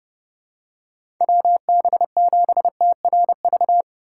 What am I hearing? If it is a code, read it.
Morse code, heard as a keyed tone, about 30 wpm, 700 Hz. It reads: WB7TRV
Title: USA Call Signs - 4x Character Spacing - 30wpm